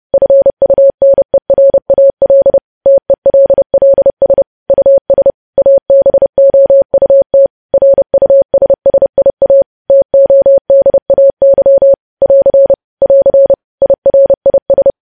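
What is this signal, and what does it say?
Morse code, heard as a keyed tone, about 30 wpm, 550 Hz. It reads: FUNERAL TELLS US ABOUT RUSSIA TODAY <AR> <AR> IRIS
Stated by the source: Morse code practice of news headlines at 30.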